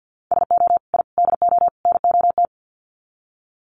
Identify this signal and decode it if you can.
Morse code, heard as a keyed tone, about 50 wpm, 700 Hz. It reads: SO I DO NOT
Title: Sets of 4 Words 50wpm